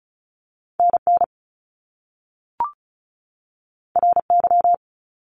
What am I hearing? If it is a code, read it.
Morse code, heard as a keyed tone, about 36 wpm, 700 Hz. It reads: NN RY
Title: Top Two Letter Combinations in Words 35wpm